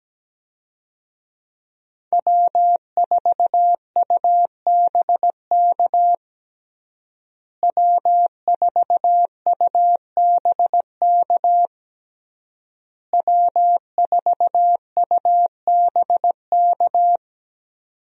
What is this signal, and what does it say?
Morse code, heard as a keyed tone, about 17 wpm, 700 Hz. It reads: W4UBK W4UBK W4UBK
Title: USA Call Signs - 5 Characters - Repeated 3x - 17wpm